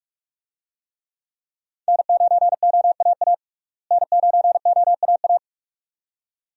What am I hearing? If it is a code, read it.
Morse code, heard as a keyed tone, about 41 wpm, 700 Hz. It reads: N9OAA N9OAA
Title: USA Call Signs - 5 Characters - Repeated 2x - 45wpm